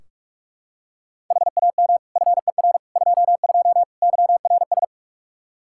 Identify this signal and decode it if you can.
Morse code, heard as a keyed tone, about 45 wpm, 700 Hz. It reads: HAM FER 22 YRS
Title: Practice QSO - Exchange 3 - More Info - 45wpm